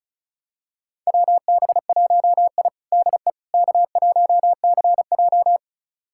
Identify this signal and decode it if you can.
Morse code, heard as a keyed tone, about 34 wpm, 700 Hz. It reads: WB1I DE K1CJ
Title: USA Callsign DE Callsign - 4 Characters - 35wpm